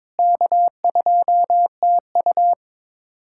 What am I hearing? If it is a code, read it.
Morse code, heard as a keyed tone, about 22 wpm, 700 Hz. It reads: K2TU
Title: USA Callsign DE Callsign - 4 Characters - 22wpm